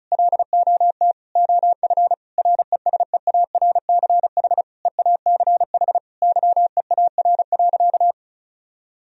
Morse code, heard as a keyed tone, about 36 wpm, 700 Hz. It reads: LOT OF RESEARCH EACH YEAR.
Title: Sentences from Top 1000 Words - Review Part 4 - 35wpm